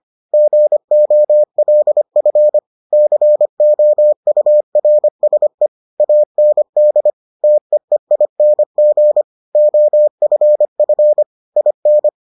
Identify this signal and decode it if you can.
Morse code, heard as a keyed tone, about 25 wpm, 600 Hz. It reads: GOLF COURSE AND TEEING OFF IN